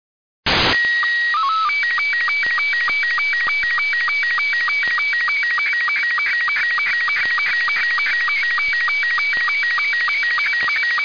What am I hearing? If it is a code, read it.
SSTV (slow-scan television), Robot 72.
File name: UmKA-1 SSTV Audio RB72 Igor PU4ELT